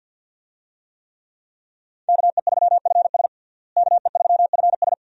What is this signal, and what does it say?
Morse code, heard as a keyed tone, about 48 wpm, 700 Hz. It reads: KE3FS KE3FS
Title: USA Call Signs - 5 Characters - Repeated 2x - 50wpm